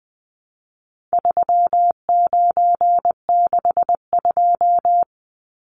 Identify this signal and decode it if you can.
Morse code, heard as a keyed tone, about 20 wpm, 700 Hz. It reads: 3962
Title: Numbers - 4-Digit 20wpm